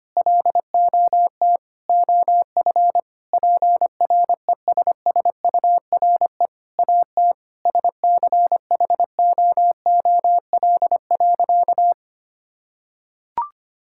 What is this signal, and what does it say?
Morse code, heard as a keyed tone, about 25 wpm, 700 Hz. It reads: LOT OF PRESSURE AT SCHOOL.
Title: Sentences from Top 1000 Words - Review Part 6 - 25wpm